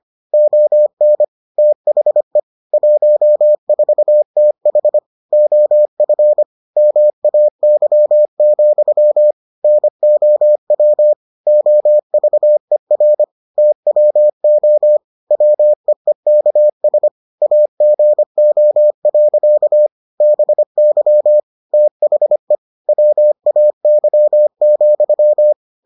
Morse code, heard as a keyed tone, about 25 wpm, 600 Hz. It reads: ON THE 14TH OF MAY, NOW OVER TWO WEEKS AGO. BY THE WAY,